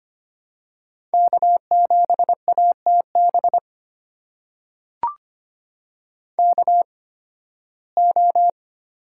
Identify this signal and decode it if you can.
Morse code, heard as a keyed tone, about 25 wpm, spaced as for 16 wpm, 700 Hz. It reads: K7ATB KO
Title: USA Call Signs - 5 Characters - 8x Character Spacing - 25wpm